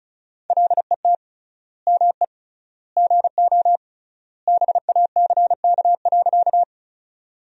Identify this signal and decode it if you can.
Morse code, heard as a keyed tone, about 36 wpm, 700 Hz. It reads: LET ME GO BACK.
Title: 3x Word-Spacing - Sentences from Top 300 Words - 35wpm